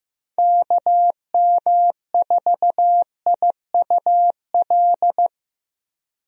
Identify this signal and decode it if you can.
Morse code, heard as a keyed tone, about 15 wpm, 700 Hz. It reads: KM4IUL
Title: USA Call Signs - 6 Characters - 15wpm